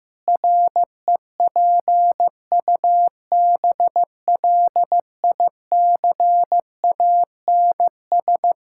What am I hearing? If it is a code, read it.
Morse code, heard as a keyed tone, about 15 wpm, 700 Hz. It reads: REPUBLICANS
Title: Pangram Challenge v2 - 15wpm